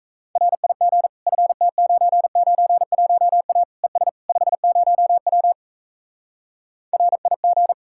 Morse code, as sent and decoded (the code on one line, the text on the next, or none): .-. .. --. / ..-. - ----. ----. .---- .- / . ... / ..... ----- .-- / .-. .. --.
RIG FT991A ES 50W RIG